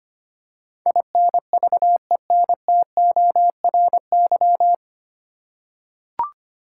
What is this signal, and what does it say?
Morse code, heard as a keyed tone, about 25 wpm, 700 Hz. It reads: INVENTORY
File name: Nine Letter Words - 25wpm